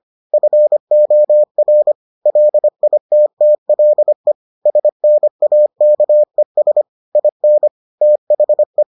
Morse code, heard as a keyed tone, about 25 wpm, 600 Hz. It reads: FOR LITTLE SNAKES IN THE